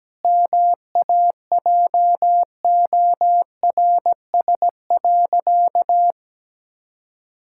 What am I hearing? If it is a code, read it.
Morse code, heard as a keyed tone, about 17 wpm, 700 Hz. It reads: MAJORS.
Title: Pangram Challenge v2 - 17wpm